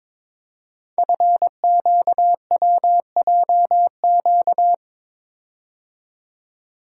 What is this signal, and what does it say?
Morse code, heard as a keyed tone, about 22 wpm, 700 Hz. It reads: FQWJQ
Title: Code Groups - 5 Letters - 22wpm